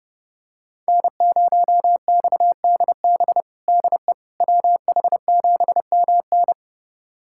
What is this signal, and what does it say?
Morse code, heard as a keyed tone, about 30 wpm, 700 Hz. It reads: N0XDB DE WH7MN
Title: USA Callsign DE Callsign - 5 Characters - 30wpm